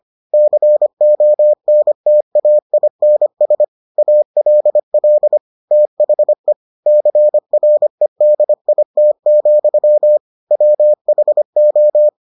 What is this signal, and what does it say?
Morse code, heard as a keyed tone, about 25 wpm, 600 Hz. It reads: CONTAINS ALL THE CREDIT, WHO